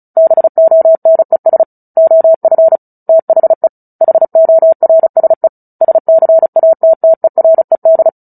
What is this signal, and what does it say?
Morse code, heard as a keyed tone, about 34 wpm, 650 Hz. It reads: BONES OF THE HORSE SCATTERED
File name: Book Two / 09-Chapter 9 - 35 WPM